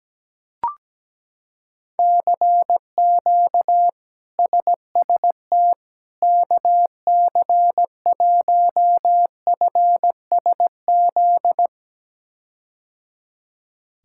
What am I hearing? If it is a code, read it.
Morse code, heard as a keyed tone, about 17 wpm, 700 Hz. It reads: CQ SST KC1FSZ
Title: K1USN SST QSO Practice - 17wpm